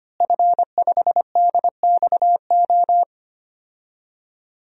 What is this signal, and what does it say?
Morse code, heard as a keyed tone, about 25 wpm, 700 Hz. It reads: F5DXO
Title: DXCC-100 Call Signs - 5 Characters - Repeated 3x - 25wpm